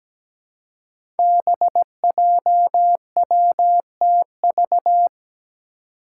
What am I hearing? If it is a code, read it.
Morse code, heard as a keyed tone, about 17 wpm, 700 Hz. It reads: BJWTV
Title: Five Challenging Letter-Number Combinations 17wpm